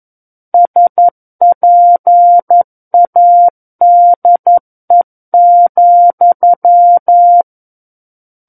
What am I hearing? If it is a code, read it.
Morse code, heard as a keyed tone, about 11 wpm, 700 Hz. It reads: SPADE,
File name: 11 WPM